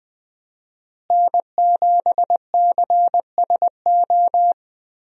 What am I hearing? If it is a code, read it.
Morse code, heard as a keyed tone, about 21 wpm, 700 Hz. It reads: N7CSO
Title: USA Call Signs - 5 Characters - Repeated 3x - 20wpm